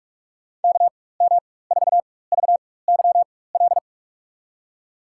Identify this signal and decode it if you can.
Morse code, heard as a keyed tone, about 42 wpm, spaced as for 20 wpm, 700 Hz. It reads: KM4VYL
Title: USA Call Signs - 6 Characters - 4x Character Spacing - 45wpm